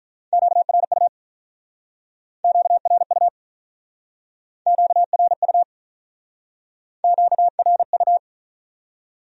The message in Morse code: QRU QRU QRU QRU

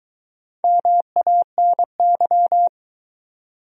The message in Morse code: MANY

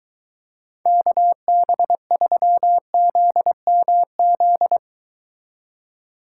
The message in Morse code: KB3ZMZ

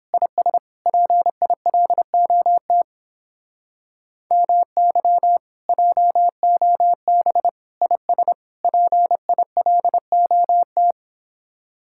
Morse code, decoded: IS PILOT MY JOB IS PILOT

